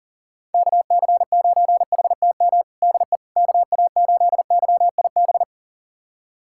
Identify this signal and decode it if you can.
Morse code, heard as a keyed tone, about 37 wpm, 700 Hz. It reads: KC9HTM DE KA8YIB
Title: USA Callsign DE Callsign 40wpm